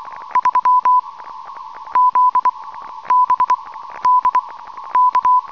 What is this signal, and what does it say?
Morse code, heard as a keyed tone, about 24 wpm, 1000 Hz. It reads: 3 Z B D K
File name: attached file / 22-31-23 cw 5.167,5 kHz